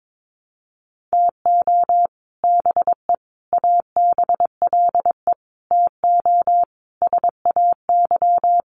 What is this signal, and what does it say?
Morse code, heard as a keyed tone, about 22 wpm, 700 Hz. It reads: TO BE ABLE TO SAY
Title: Sets of 5 Words Encore2 22wpm